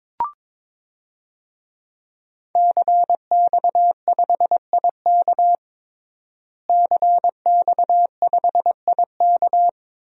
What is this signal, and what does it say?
Morse code, heard as a keyed tone, about 22 wpm, 700 Hz. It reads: CX5IK CX5IK
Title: DXCC-100 Call Signs - 5 Characters - Repeated 2x - 22wpm